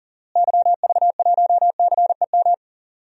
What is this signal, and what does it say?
Morse code, heard as a keyed tone, about 37 wpm, 700 Hz. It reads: YV1CEM
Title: DXCC-100 Call Signs - 6 Characters - Repeated 3x - 40wpm